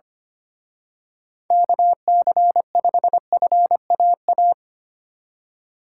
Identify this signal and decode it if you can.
Morse code, heard as a keyed tone, about 29 wpm, 700 Hz. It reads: KC5FAA
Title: Speed Racing - USA 2x3 Call Signs 25wpm